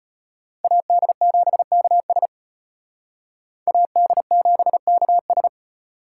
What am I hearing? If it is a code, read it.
Morse code, heard as a keyed tone, about 32 wpm, 700 Hz. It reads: AD7KS AD7KS